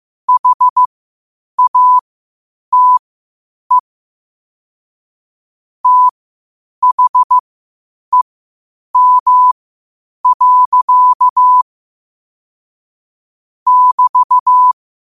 Morse code, heard as a keyed tone, about 15 wpm, spaced as for 8 wpm, 1000 Hz. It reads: HATE THEM. <BT>
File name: qotdfiles / 8wpm